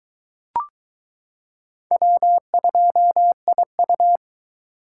Morse code, heard as a keyed tone, about 21 wpm, 700 Hz. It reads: W2IU